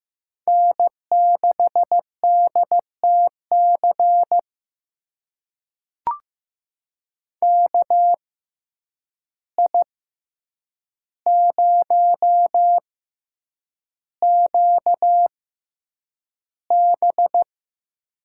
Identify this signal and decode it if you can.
Morse code, heard as a keyed tone, about 15 wpm, spaced as for 8 wpm, 700 Hz. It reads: N6DTC KI0QB
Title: USA Call Signs - 5 Characters - 6x Character Spacing - 15wpm